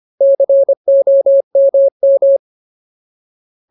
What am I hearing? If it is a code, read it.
Morse code, heard as a keyed tone, about 25 wpm, 550 Hz. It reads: COMM
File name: Ultimate Word List / 25WPM Ultimate